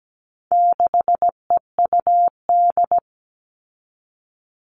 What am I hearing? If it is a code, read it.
Morse code, heard as a keyed tone, about 17 wpm, 700 Hz. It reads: 6EUD
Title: All Cut Numbers - 4-Digit 17wpm